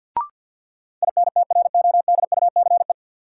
Morse code, heard as a keyed tone, about 53 wpm, 700 Hz. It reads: INTRODUCE
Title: Sentences from Top 1600 Words - New Word Drill Alt - 50wpm